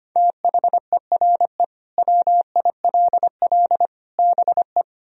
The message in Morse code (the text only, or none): THERE WILL BE